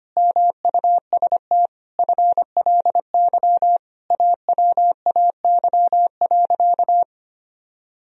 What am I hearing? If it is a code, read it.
Morse code, heard as a keyed tone, about 25 wpm, 700 Hz. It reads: MUST FLY AWAY.
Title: Sentences from Top 1000 Words - Review Part 2 - 25wpm